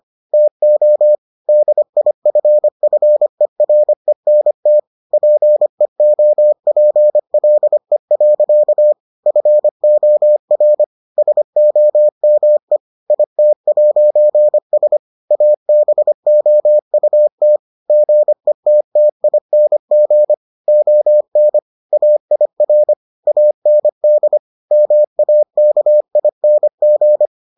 Morse code, heard as a keyed tone, about 25 wpm, 600 Hz. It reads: TO DIFFERENT PEOPLE. FOR SOME IT'S ABOUT GETTING ON AIR AND MAKING